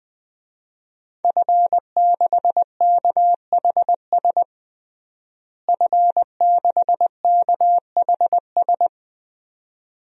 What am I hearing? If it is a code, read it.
Morse code, heard as a keyed tone, about 21 wpm, 700 Hz. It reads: F6KHS F6KHS
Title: DXCC-100 Call Signs - 5 Characters - Repeated 2x - 20wpm